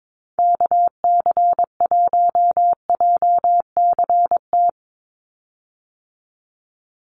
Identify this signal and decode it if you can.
Morse code, heard as a keyed tone, about 22 wpm, 700 Hz. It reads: KC1JCT
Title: USA Call Signs 22wpm